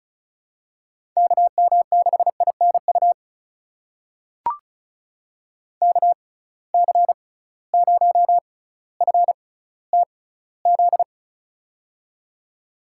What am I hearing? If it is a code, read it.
Morse code, heard as a keyed tone, about 36 wpm, spaced as for 18 wpm, 690 Hz. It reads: KM6INU KC0FTZ
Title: USA Call Signs - 6 Characters - 6x Character Spacing - 35wpm